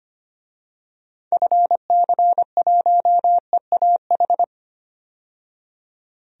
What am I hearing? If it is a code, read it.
Morse code, heard as a keyed tone, about 25 wpm, 700 Hz. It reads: FC1EAH
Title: DXCC-100 Call Signs - 6 Characters - 25wpm